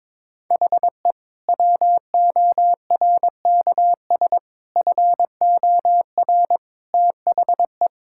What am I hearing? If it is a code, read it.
Morse code, heard as a keyed tone, about 22 wpm, 700 Hz. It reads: HE WORKS FOR THE
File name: practice / 22wpm / Sentences with 2-Word Phrases - 22wpm